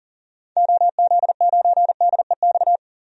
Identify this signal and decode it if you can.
Morse code, heard as a keyed tone, about 40 wpm, 700 Hz. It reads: OZ9DEX
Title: DXCC-100 Call Signs - 6 Characters - 40wpm